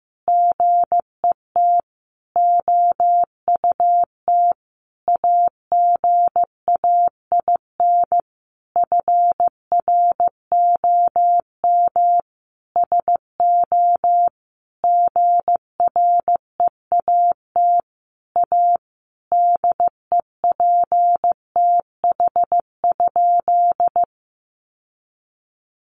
Morse code, heard as a keyed tone, about 15 wpm, 700 Hz. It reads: GET OUT AGAIN FROM SO GREAT A DEPTH?
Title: Sentences from Top 1200 Words 15wpm